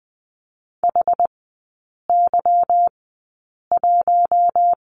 Morse code, heard as a keyed tone, about 20 wpm, spaced as for 8 wpm, 700 Hz. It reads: HY1
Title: 145 Lesson 24e - Letter y - Three Characters 20wpm